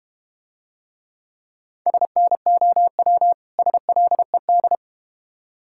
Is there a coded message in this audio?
Yes: Morse code from 1.9 to 4.8 s, heard as a keyed tone, about 32 wpm, 700 Hz. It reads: SNOW SLED